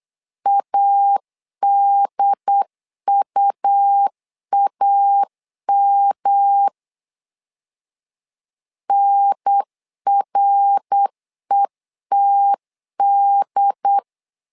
Morse code, heard as a keyed tone, about 8 wpm, 800 Hz. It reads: ADUAM NRETD
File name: LEZIONI / 3 MWRO / 3 MWR 8wpm 2444